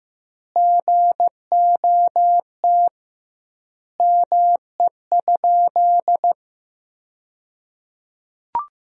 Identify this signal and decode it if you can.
Morse code, heard as a keyed tone, about 15 wpm, 700 Hz. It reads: GOT ME?
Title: 2x Word-Spacing - Sentences from Top 300 Words - 15wpm